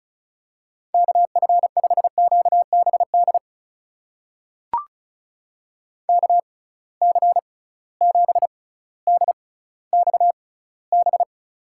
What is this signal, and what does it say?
Morse code, heard as a keyed tone, about 36 wpm, spaced as for 18 wpm, 700 Hz. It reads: KF5QBD KC7DXB
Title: USA Call Signs - 6 Characters - 6x Character Spacing - 35wpm